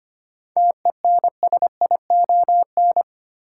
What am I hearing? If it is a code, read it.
Morse code, heard as a keyed tone, about 25 wpm, 700 Hz. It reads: TENSION
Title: Sentences from Top 1600 Words - New Word Drill Alt - 25wpm